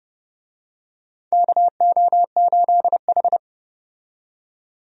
Morse code, heard as a keyed tone, about 30 wpm, 700 Hz. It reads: KO8H
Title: USA Call Signs - 4 Characters - 30wpm